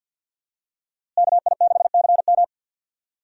Morse code, heard as a keyed tone, about 52 wpm, 700 Hz. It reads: KI6CM
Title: USA Call Signs - 5 Characters - Repeated 3x - 50wpm